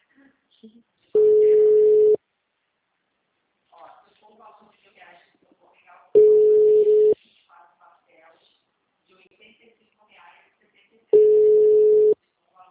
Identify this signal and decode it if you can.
Ringback tone (Europe, 425 Hz), with speech.